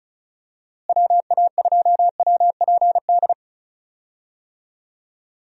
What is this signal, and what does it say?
Morse code, heard as a keyed tone, about 36 wpm, 700 Hz. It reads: WA2WPD
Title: USA Call Signs - 6 Characters - Repeated 2x - 35wpm